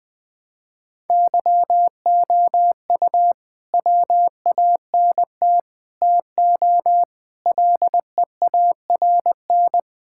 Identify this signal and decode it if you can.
Morse code, heard as a keyed tone, about 20 wpm, 700 Hz. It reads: YOU WANT TO LEARN
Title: Sets of 6 Words - 20wpm